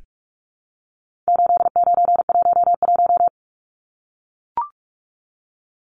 Morse code, heard as a keyed tone, about 44 wpm, 700 Hz. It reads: 8911
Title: Numbers - 4-Digit 45wpm